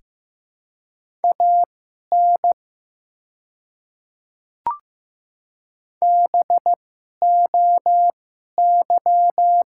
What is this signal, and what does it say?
Morse code, heard as a keyed tone, about 15 wpm, spaced as for 11 wpm, 700 Hz. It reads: AN BOY
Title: 2x Character Spacing - Top 300 Words - 15wpm